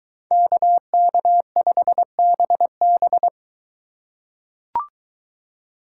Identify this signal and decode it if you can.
Morse code, heard as a keyed tone, about 21 wpm, 700 Hz. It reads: KK5BB